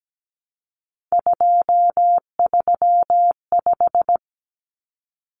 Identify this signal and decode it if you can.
Morse code, heard as a keyed tone, about 17 wpm, 700 Hz. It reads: 235
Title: Common Cut Numbers - 3-Digit 17wpm